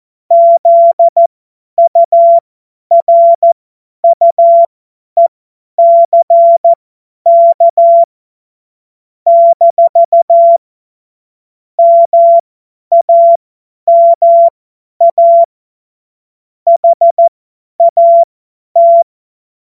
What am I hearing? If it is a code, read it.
Morse code, heard as a keyed tone, about 14 wpm, spaced as for 10 wpm, 680 Hz. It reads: ZURUECK - MAMA HAT
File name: Morsekurs / Witze / Witz 16. Februar 2021, 50 F70